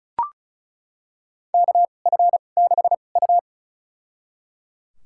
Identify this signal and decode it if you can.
Morse code, heard as a keyed tone, about 34 wpm, spaced as for 25 wpm, 700 Hz. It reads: KF6U
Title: USA Call Signs - 4 Characters - 2x Character Spacing - 35wpm